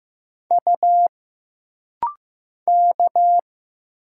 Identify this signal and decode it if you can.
Morse code, heard as a keyed tone, about 15 wpm, 700 Hz. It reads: U K
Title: Single Letters - Rapid-Fire 15wpm